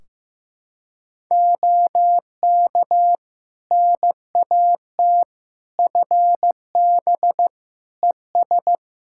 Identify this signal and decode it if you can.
Morse code, heard as a keyed tone, about 16 wpm, 700 Hz. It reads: OK NAT FB ES
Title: Practice QSO - Exchange 2 - Info - 15wpm